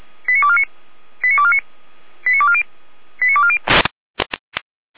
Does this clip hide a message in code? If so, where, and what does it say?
Yes: five-tone selective call (ZVEI) from 0.3 to 3.6 s, heard as quick runs of five tones, tones of about 70 ms. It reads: 89249 89249 89240 89240